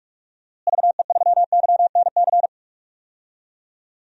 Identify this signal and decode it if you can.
Morse code, heard as a keyed tone, about 46 wpm, 700 Hz. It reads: VE3YNC
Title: DXCC-100 Call Signs - 6 Characters - Repeated 2x - 45wpm